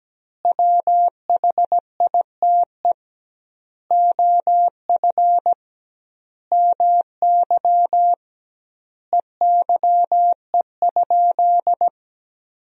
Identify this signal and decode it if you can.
Morse code, heard as a keyed tone, about 17 wpm, 700 Hz. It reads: WHITE OF MY EYE?
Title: 2x Word-Spacing - Sentences from Top 500 Words - 17wpm